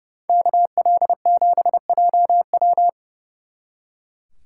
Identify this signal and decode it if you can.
Morse code, heard as a keyed tone, about 36 wpm, 700 Hz. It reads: KL7JW
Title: Speed Racing - USA 2x2 Call Signs 30wpm